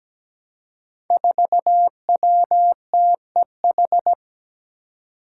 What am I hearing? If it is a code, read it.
Morse code, heard as a keyed tone, about 17 wpm, 700 Hz. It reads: 4WTEH
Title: Five Challenging Letter-Number Combinations 17wpm